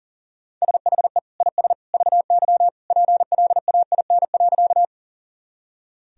Morse code, heard as a keyed tone, about 37 wpm, 700 Hz. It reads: SHE IS VY PLAIN.
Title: Sentences from Top 400 Words and Abbreviations 40wpm